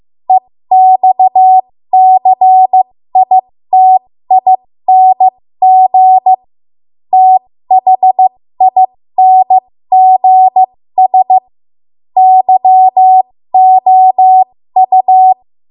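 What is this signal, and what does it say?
Morse code, heard as a keyed tone, about 15 wpm, spaced as for 13 wpm, 750 Hz. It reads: EXCITING THINGS YOU
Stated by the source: 13 wpm Morse code practice archive transmission.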